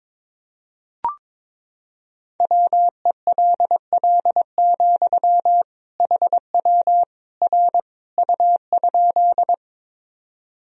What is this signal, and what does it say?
Morse code, heard as a keyed tone, about 22 wpm, 700 Hz. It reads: WELL, HW R U?
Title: Sentences from Top 200 Words and Abbreviations 22wpm